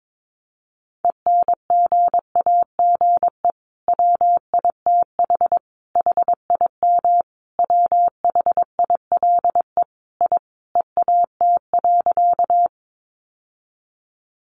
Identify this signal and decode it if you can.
Morse code, heard as a keyed tone, about 22 wpm, 700 Hz. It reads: ENGAGE WITH HIM WHILE I EAT.
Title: Sentences from Top 1500 Words 22wpm